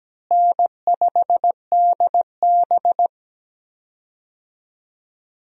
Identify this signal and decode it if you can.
Morse code, heard as a keyed tone, about 17 wpm, 700 Hz. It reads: N5DB
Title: USA Call Signs - 4 Characters - Repeated 3x - 17wpm